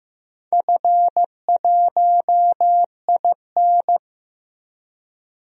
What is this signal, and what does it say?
Morse code, heard as a keyed tone, about 15 wpm, 700 Hz. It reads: F1IN
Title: DXCC-100 Call Signs - 4 Characters - 15wpm